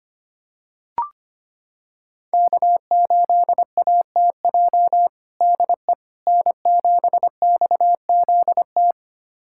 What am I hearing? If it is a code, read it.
Morse code, heard as a keyed tone, about 25 wpm, 700 Hz. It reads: K8ATJ DE N7XZT
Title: USA Callsign DE Callsign - 5 Characters - 25wpm